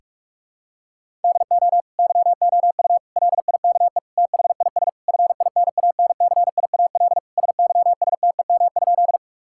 Morse code, heard as a keyed tone, about 46 wpm, 700 Hz. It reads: DO YOU LIKE THIS FINANCIAL SYSTEM?
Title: Sentences from Top 1000 Words - Review Part 5 - 45wpm